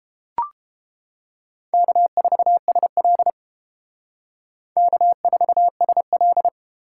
Morse code, heard as a keyed tone, about 29 wpm, 700 Hz. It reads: K4SL K4SL